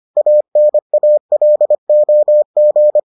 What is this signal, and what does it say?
Morse code, heard as a keyed tone, about 25 wpm, 600 Hz. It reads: ANALOG